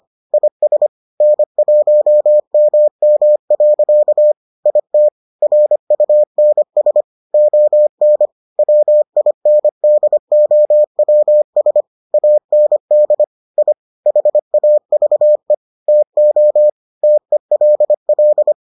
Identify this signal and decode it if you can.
Morse code, heard as a keyed tone, about 25 wpm, 600 Hz. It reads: IS N1MM. IT RUNS ON WINDOWS AND I HAVE TO TELL